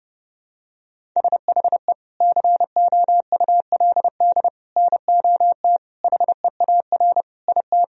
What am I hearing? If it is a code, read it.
Morse code, heard as a keyed tone, about 32 wpm, 700 Hz. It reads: SHE COULD NOT HEAR IT
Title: Sentences from Top 1000 Words - Review Part 2 - 30wpm